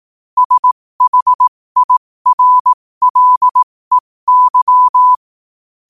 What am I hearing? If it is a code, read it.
Morse code, heard as a keyed tone, about 18 wpm, spaced as for 15 wpm, 1000 Hz. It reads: SHIRLEY
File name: qotdfiles / 15wpm